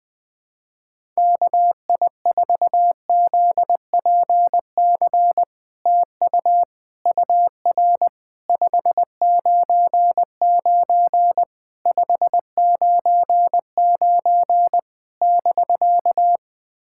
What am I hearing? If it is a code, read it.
Morse code, heard as a keyed tone, about 21 wpm, 700 Hz. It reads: KI4ZPC TU UR 599 599 <BK>
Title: SOTA QSO Practice 20wpm